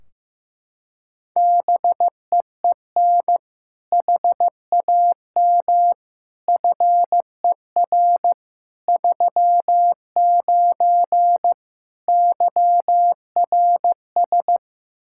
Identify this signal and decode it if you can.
Morse code, heard as a keyed tone, about 15 wpm, 700 Hz. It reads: BEEN HAM FER 39 YRS
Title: Practice QSO - Exchange 3 - More Info - 15wpm